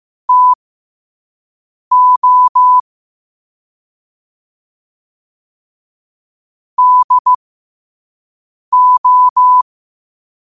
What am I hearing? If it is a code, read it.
Morse code, heard as a keyed tone, about 15 wpm, spaced as for 5 wpm, 1000 Hz. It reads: TO DO
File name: qotdfiles / 5wpm